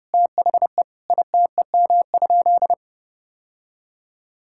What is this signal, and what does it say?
Morse code, heard as a keyed tone, about 32 wpm, 700 Hz. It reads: THE ITEM?
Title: Sentences from Top 1000 Words - Review Part 6 - 30wpm